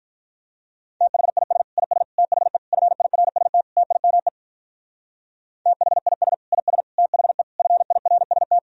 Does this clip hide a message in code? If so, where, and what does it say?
Yes: Morse code from 1.0 to 8.6 s, heard as a keyed tone, about 45 wpm, 700 Hz. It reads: THIS IS THE FIRST TIME THIS IS THE FIRST